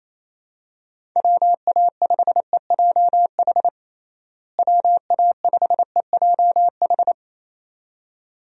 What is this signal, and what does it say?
Morse code, heard as a keyed tone, about 28 wpm, 700 Hz. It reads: WA5EJH WA5EJH
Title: USA Call Signs - 6 Characters - Repeated 2x - 28wpm